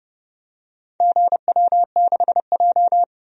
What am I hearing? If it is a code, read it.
Morse code, heard as a keyed tone, about 30 wpm, 700 Hz. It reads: GW6J
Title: DXCC-100 Call Signs - 4 Characters - 30wpm